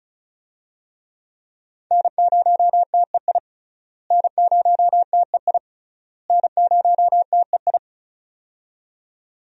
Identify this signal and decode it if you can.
Morse code, heard as a keyed tone, about 36 wpm, 700 Hz. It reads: N0TEI N0TEI N0TEI
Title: USA Call Signs - 5 Characters - Repeated 3x - 35wpm